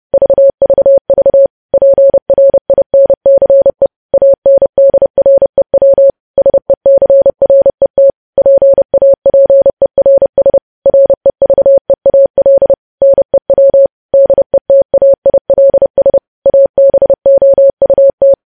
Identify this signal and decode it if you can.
Morse code, heard as a keyed tone, about 30 wpm, 550 Hz. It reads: VVV PRINCE ANDREW SECRET PAPERS REVEAL NEW DETAILS ABOUT
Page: news headlines in morse code at 30 wpm